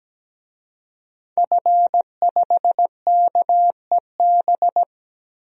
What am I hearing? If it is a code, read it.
Morse code, heard as a keyed tone, about 17 wpm, 700 Hz. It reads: F5KEB
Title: DXCC-100 Call Signs - 5 Characters - 17wpm